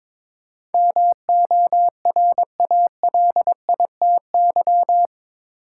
Morse code, heard as a keyed tone, about 22 wpm, 700 Hz. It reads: MORALITY
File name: practice / 22wpm / Eight Letter Words - 22wpm